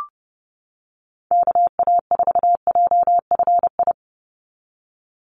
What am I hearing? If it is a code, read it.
Morse code, heard as a keyed tone, about 30 wpm, 700 Hz. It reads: KA4JFI
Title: USA Call Signs 30wpm